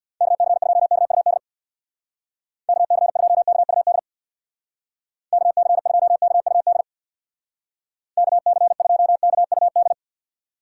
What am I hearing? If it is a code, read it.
Morse code, heard as a keyed tone, about 60 wpm, 700 Hz. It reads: KC2KUD KC2KUD KC2KUD KC2KUD
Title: Speed Racing - USA 2x3 Call Signs 50wpm